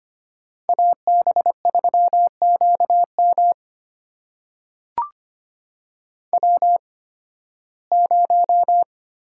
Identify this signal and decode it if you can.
Morse code, heard as a keyed tone, about 25 wpm, spaced as for 16 wpm, 700 Hz. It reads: AB3QM W0
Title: USA Call Signs - 5 Characters - 8x Character Spacing - 25wpm